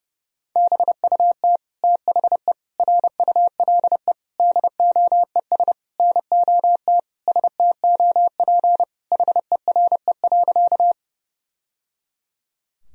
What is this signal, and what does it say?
Morse code, heard as a keyed tone, about 28 wpm, 700 Hz. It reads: BUT THE RULE DOES NOT STOP HERE.
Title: Sentences from Top 1000 Words - Review Part 2 - 30wpm